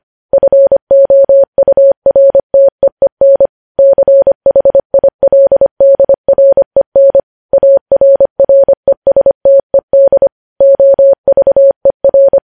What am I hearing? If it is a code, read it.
Morse code, heard as a keyed tone, about 25 wpm, 550 Hz. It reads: FOURTEEN CHILDREN ARRESTED OVER
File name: podcast / News headlines-1x-25x25